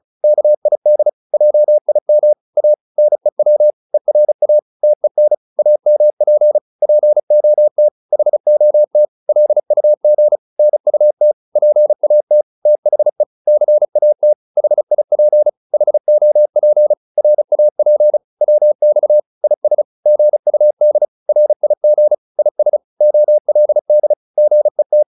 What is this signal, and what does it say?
Morse code, heard as a keyed tone, about 35 wpm, 600 Hz. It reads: KID JIM A NEW ERA TEN AMP POT HOT LUG NUT PAT THE CAT HIP HOP RAP WX IS GUD RIG IS OLD GET